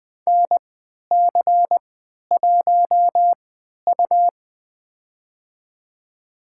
Morse code, heard as a keyed tone, about 20 wpm, spaced as for 11 wpm, 700 Hz. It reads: NC1U
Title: USA Call Signs - 4 Characters - 3x Character Spacing - 20wpm